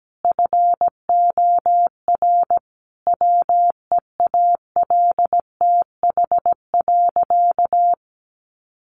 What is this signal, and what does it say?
Morse code, heard as a keyed tone, about 17 wpm, 700 Hz. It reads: FOR WEALTH.
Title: Sentences from Top 1200 Words 17wpm